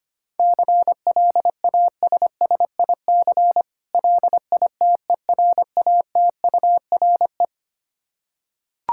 Morse code, heard as a keyed tone, about 25 wpm, 700 Hz. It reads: CLASSIC LITERATURE
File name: practice / 25wpm / Difficult Words with Hints - 25wpm